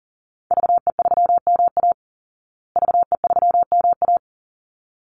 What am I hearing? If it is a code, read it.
Morse code, heard as a keyed tone, about 40 wpm, 700 Hz. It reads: VE3MA VE3MA
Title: Call Signs Repeated 3x - 40wpm